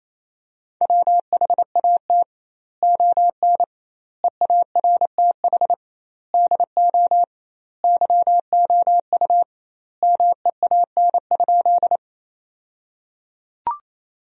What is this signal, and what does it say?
Morse code, heard as a keyed tone, about 28 wpm, 700 Hz. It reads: WHAT ON EARTH DO YOU MEAN?
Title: 2x Word-Spacing - Sentences from Top 200 Words - 28wpm